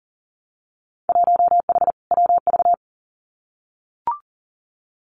Morse code, heard as a keyed tone, about 42 wpm, 700 Hz. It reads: 1H WV
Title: Winter Field Day Exchanges 40wpm